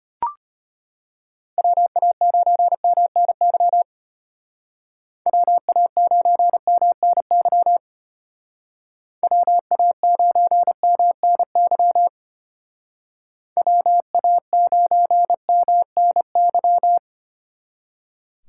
Morse code, heard as a keyed tone, about 30 wpm, 700 Hz. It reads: WA9MNY WA9MNY WA9MNY WA9MNY
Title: Speed Racing - USA 2x3 Call Signs 25wpm